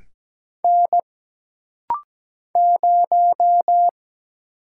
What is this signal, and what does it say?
Morse code, heard as a keyed tone, about 17 wpm, 700 Hz. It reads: N 0
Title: Single Letter-Number - Rapid-Fire 17wpm